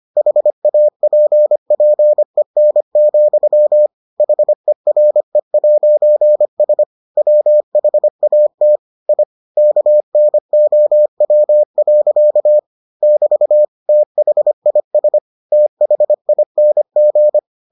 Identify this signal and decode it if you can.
Morse code, heard as a keyed tone, about 25 wpm, 600 Hz. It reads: HAPPEN, HERE'S WHAT I KNOW. <BT> THIS THING